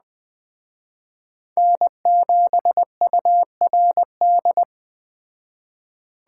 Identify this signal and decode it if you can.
Morse code, heard as a keyed tone, about 20 wpm, 700 Hz. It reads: N7URD
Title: Speed Racing - USA 1x3 Call Signs 17wpm